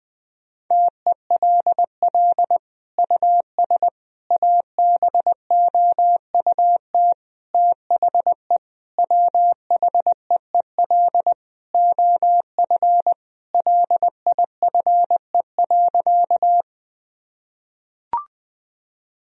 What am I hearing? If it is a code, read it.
Morse code, heard as a keyed tone, about 21 wpm, 700 Hz. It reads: TELL US ABOUT THE WHEEL OF LIFE.
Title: Sentences from Top 1000 Words - Review Part 3 - 20wpm